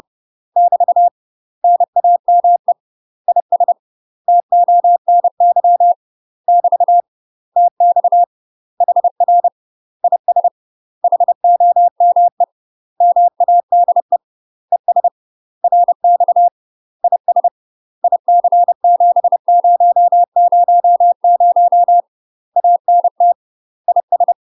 Morse code, heard as a keyed tone, about 30 wpm, 700 Hz. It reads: <BT> NAME IS TONY <BT> TX HR IS HOME MADE ES RX IS IC7000 ANT IS
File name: UK MASTER QSOs TRAINING / 30WPM / UK MASTER QSOs 0050